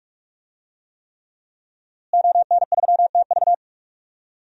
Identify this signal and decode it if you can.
Morse code, heard as a keyed tone, about 44 wpm, 700 Hz. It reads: ON3TV
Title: DXCC-100 Call Signs - 5 Characters - Repeated 3x - 45wpm